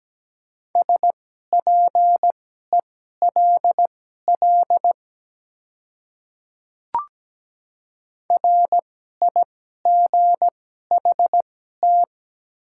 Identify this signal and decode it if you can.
Morse code, heard as a keyed tone, about 17 wpm, spaced as for 12 wpm, 690 Hz. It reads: SPELL RIGHT
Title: 2x Character Spacing - Top 300 Words - 17wpm